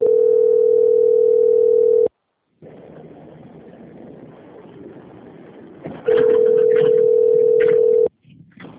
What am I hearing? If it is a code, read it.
Ringback tone (North America), 440 + 480 Hz.